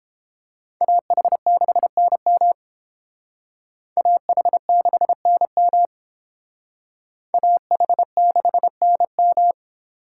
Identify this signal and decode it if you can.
Morse code, heard as a keyed tone, about 29 wpm, 700 Hz. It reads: AH6NM AH6NM AH6NM